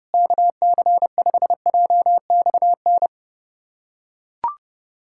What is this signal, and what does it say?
Morse code, heard as a keyed tone, about 30 wpm, 700 Hz. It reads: KC5JXN
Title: USA Callsign DE Callsign 30wpm